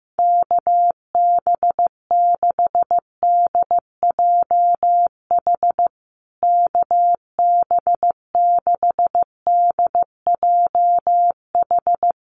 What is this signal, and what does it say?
Morse code, heard as a keyed tone, about 15 wpm, 700 Hz. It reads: KB6DJH KB6DJH
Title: USA Call Signs Repeated 3x 15wpm